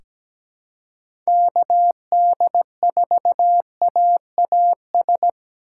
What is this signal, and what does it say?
Morse code, heard as a keyed tone, about 17 wpm, 700 Hz. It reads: KD4AAS
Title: USA Call Signs - 2x Character Spacing - 17wpm